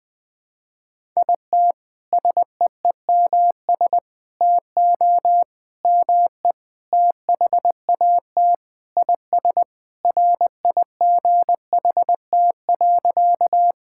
Morse code, heard as a keyed tone, about 21 wpm, 700 Hz. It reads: IT SEEMS TO ME THAT IS RIGHT.
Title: Sentences with 4-Word Phrases - 20wpm